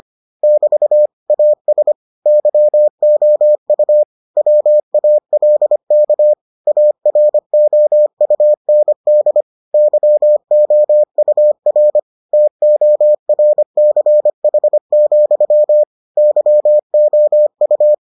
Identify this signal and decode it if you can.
Morse code, heard as a keyed tone, about 25 wpm, 600 Hz. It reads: <BT> AS YOU WALK AROUND YOUR TORCH, YOU